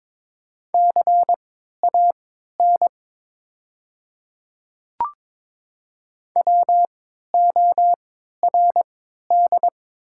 Morse code, heard as a keyed tone, about 22 wpm, spaced as for 12 wpm, 700 Hz. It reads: CAN WORD
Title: 3x Character Spacing - Top 100 Words - 22wpm